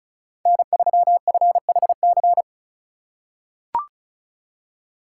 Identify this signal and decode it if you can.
Morse code, heard as a keyed tone, about 36 wpm, 700 Hz. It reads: N3FHC
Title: USA Call Signs - 5 Characters - 3x Character Spacing - 35wpm